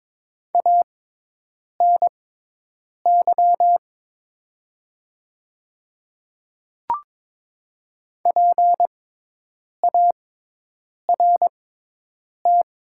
Morse code, heard as a keyed tone, about 22 wpm, spaced as for 8 wpm, 700 Hz. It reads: ANY PART
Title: 6x Character Spacing - Top 200 Words - 22wpm